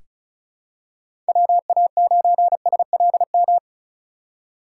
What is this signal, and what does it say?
Morse code, heard as a keyed tone, about 34 wpm, 700 Hz. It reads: WA9SLM
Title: USA Call Signs - 6 Characters - 2x Character Spacing - 35wpm